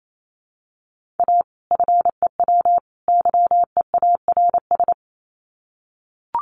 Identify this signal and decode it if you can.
Morse code, heard as a keyed tone, about 28 wpm, 700 Hz. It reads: A FEW YEARS
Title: Sets of 3 Words 28wpm